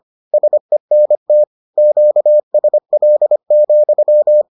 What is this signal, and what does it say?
Morse code, heard as a keyed tone, about 25 wpm, 600 Hz. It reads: SENT QSL,